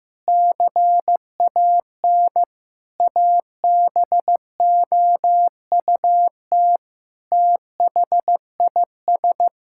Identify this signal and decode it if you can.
Morse code, heard as a keyed tone, about 15 wpm, 700 Hz. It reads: CAN ABOUT THIS
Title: Sentences from Top 1000 Words - Review Part 5 - 15wpm